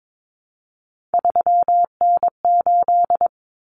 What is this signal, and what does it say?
Morse code, heard as a keyed tone, about 22 wpm, 700 Hz. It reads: 3N8
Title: Common Cut Numbers - 3-Digit 22wpm